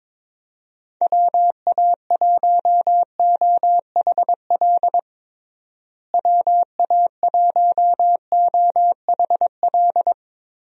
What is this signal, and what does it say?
Morse code, heard as a keyed tone, about 22 wpm, 700 Hz. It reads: WA1OHL WA1OHL
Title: USA Call Signs - 6 Characters - Repeated 2x - 22wpm